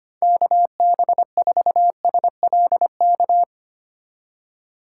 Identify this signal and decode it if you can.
Morse code, heard as a keyed tone, about 25 wpm, 700 Hz. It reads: KB4SLK
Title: USA Call Signs - 6 Characters - 8x Character Spacing - 25wpm